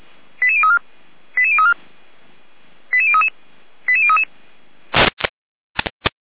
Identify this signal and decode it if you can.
Five-tone selective call (ZVEI), heard as quick runs of five tones, tones of about 70 ms. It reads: 88034 88034 88033 88033
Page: /455/SDR/ZVEI/